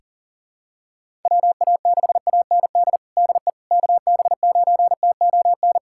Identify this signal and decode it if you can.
Morse code, heard as a keyed tone, about 40 wpm, 700 Hz. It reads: WA6AND DE KB9TON
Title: USA Callsign DE Callsign 40wpm